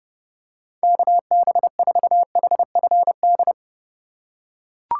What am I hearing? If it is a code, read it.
Morse code, heard as a keyed tone, about 30 wpm, 700 Hz. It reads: KB4HFD
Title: USA Call Signs - 4x Character Spacing - 30wpm